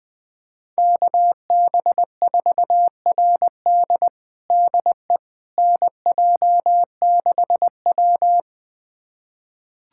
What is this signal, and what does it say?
Morse code, heard as a keyed tone, about 20 wpm, 700 Hz. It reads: KB4RD DE NJ6W
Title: USA Callsign DE Callsign 20wpm